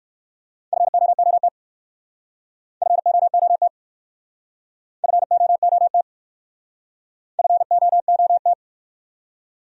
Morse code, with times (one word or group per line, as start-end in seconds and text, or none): 0.7-1.5 FOOT
2.8-3.7 FOOT
5.0-6.0 FOOT
7.4-8.5 FOOT